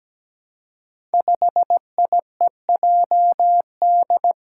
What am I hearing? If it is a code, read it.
Morse code, heard as a keyed tone, about 17 wpm, 700 Hz. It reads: 5IEJD
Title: Five Challenging Letter-Number Combinations 17wpm